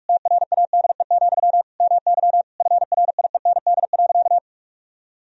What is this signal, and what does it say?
Morse code, heard as a keyed tone, about 42 wpm, 700 Hz. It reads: TRADE, MY FRIEND.
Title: Sentences from Top 1000 Words - Review Part 4 - 45wpm